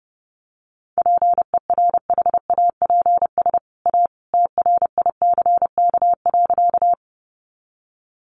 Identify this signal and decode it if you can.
Morse code, heard as a keyed tone, about 30 wpm, 700 Hz. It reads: PERHAPS A TRICK.
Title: Sentences from Top 1500 Words 30wpm